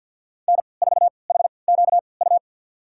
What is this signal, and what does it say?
Morse code, heard as a keyed tone, about 52 wpm, spaced as for 29 wpm, 700 Hz. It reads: N4HQU
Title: USA Call Signs - 5 Characters - 3x Character Spacing - 50wpm